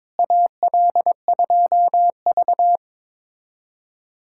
Morse code, heard as a keyed tone, about 22 wpm, 700 Hz. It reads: AL2V
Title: USA Callsign DE Callsign - 4 Characters - 22wpm